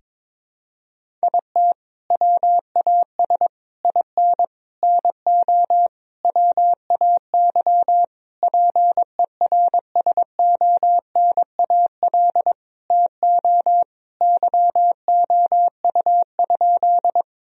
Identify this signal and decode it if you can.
Morse code, heard as a keyed tone, about 22 wpm, 700 Hz. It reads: IT WAS IN NO WAY PERSONAL TO YOU?
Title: Sentences from Top 1000 Words - Review Part 4 - 22wpm